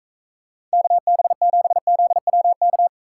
Morse code, heard as a keyed tone, about 44 wpm, 700 Hz. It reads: KB7ZWK